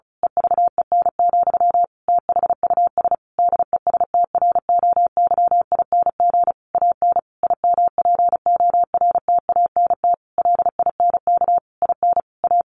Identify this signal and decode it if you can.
Morse code, heard as a keyed tone, about 34 wpm, 700 Hz. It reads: EVEN, THUS DESTROYING AN IMPORTANT LINK IN A
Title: A Sound Of Thunder 35wpm